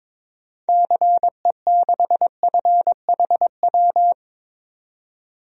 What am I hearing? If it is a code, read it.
Morse code, heard as a keyed tone, about 22 wpm, 700 Hz. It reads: CE6FHW
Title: DXCC-100 Call Signs - 6 Characters - 22wpm